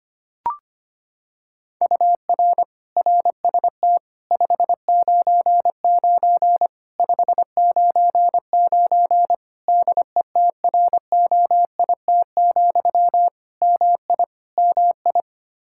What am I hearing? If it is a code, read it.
Morse code, heard as a keyed tone, about 25 wpm, 700 Hz. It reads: UR RST 599 599 DETROIT, MI MI
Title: Practice QSO - Exchange 1 - Beginning - 25wpm